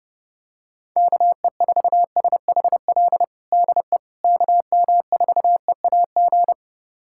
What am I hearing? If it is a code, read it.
Morse code, heard as a keyed tone, about 30 wpm, 700 Hz. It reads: KE4SHL DE KM4EAG
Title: USA Callsign DE Callsign - 6 Characters - 30wpm